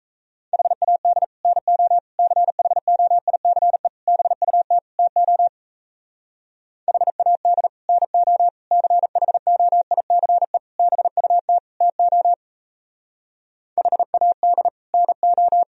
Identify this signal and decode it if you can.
Morse code, heard as a keyed tone, about 37 wpm, 700 Hz. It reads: HAD NO CHOICE BUT TO HAD NO CHOICE BUT TO HAD NO